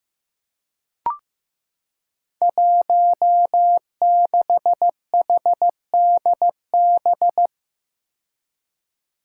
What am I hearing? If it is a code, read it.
Morse code, heard as a keyed tone, about 15 wpm, 700 Hz. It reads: 16HDB
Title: Five Challenging Letter-Number Combinations 15wpm